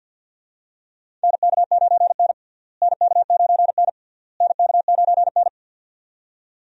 Morse code, heard as a keyed tone, about 50 wpm, 700 Hz. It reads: NK9N NK9N NK9N
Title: USA Call Signs - 4 Characters - Repeated 3x - 50wpm